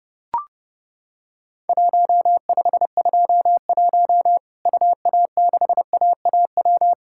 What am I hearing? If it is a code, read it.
Morse code, heard as a keyed tone, about 30 wpm, 700 Hz. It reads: 1521 UA6AAW
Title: Worked All Europe DX Contest - QTC Exchange - 30wpm